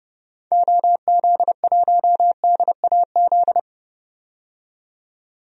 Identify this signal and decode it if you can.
Morse code, heard as a keyed tone, about 31 wpm, 700 Hz. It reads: OZ1DAZ
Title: DXCC-100 Call Signs - 6 Characters - 30wpm